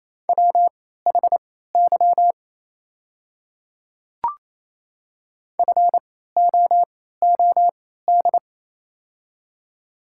Morse code, heard as a keyed tone, about 28 wpm, spaced as for 16 wpm, 700 Hz. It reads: WHY FOOD
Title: 3x Character Spacing - Top 300 Words - 28wpm